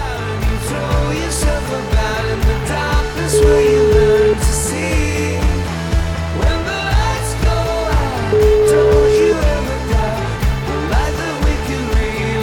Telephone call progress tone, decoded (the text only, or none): ringback tone